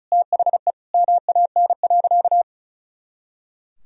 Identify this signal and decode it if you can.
Morse code, heard as a keyed tone, about 36 wpm, 700 Hz. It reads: THE MAN.
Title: Sentences from Top 200 Words and Abbreviations 35wpm